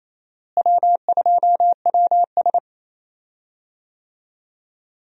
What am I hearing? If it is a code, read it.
Morse code, heard as a keyed tone, about 28 wpm, 700 Hz. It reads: W2WS
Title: USA Call Signs - 4 Characters - Repeated 3x - 28wpm